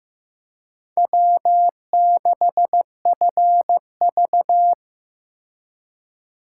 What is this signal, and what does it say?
Morse code, heard as a keyed tone, about 18 wpm, 700 Hz. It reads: W6FV
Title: Speed Racing - USA 1x2 Call Signs 15wpm